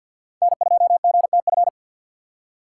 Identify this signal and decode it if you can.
Morse code, heard as a keyed tone, about 47 wpm, 700 Hz. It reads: N2GTF
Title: USA Call Signs - 5 Characters - 3x Character Spacing - 50wpm